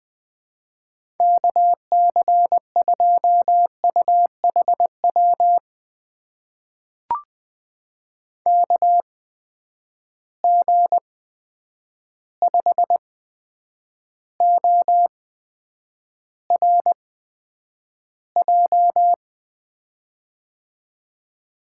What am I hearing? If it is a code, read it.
Morse code, heard as a keyed tone, about 20 wpm, spaced as for 9 wpm, 700 Hz. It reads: KC2UHW KG5ORJ
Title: USA Call Signs - 6 Characters - 8x Character Spacing - 20wpm